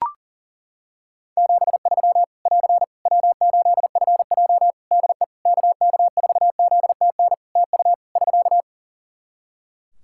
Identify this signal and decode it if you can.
Morse code, heard as a keyed tone, about 40 wpm, 700 Hz. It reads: 73 <AR> W8FJ DE KK4ZTN TU <SK>
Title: Practice QSO - Exchange 4 - Ending - 40wpm